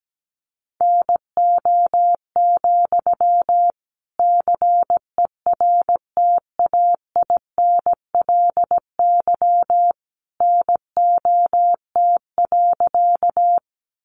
Morse code, heard as a keyed tone, about 17 wpm, 700 Hz. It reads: NO, CERTAINLY NOT.
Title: Sentences from Top 900 Words 17wpm